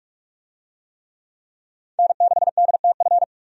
Morse code, heard as a keyed tone, about 46 wpm, 700 Hz. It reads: N6DTF
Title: USA Call Signs - 5 Characters - 45wpm